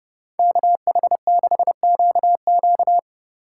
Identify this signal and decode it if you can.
Morse code, heard as a keyed tone, about 30 wpm, 700 Hz. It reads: KH6QQ